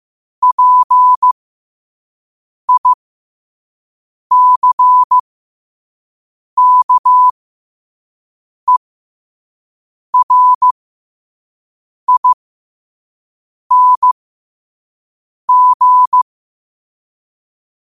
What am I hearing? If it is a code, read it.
Morse code, heard as a keyed tone, about 15 wpm, spaced as for 5 wpm, 1000 Hz. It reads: PICKERING
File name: qotdfiles / 5wpm